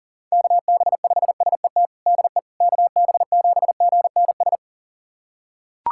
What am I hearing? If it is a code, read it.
Morse code, heard as a keyed tone, about 40 wpm, 700 Hz. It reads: KB5SET DE KB7GNS
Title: USA Callsign DE Callsign - 6 Characters - 40wpm